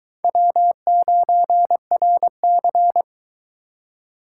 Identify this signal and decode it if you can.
Morse code, heard as a keyed tone, about 22 wpm, 700 Hz. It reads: W9RC